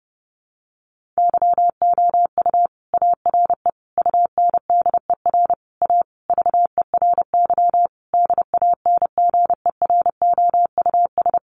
Morse code, heard as a keyed tone, about 30 wpm, 700 Hz. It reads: YOU ARE UNDER A VERY DANGEROUS
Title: Sentences from Top 1200 Words 30wpm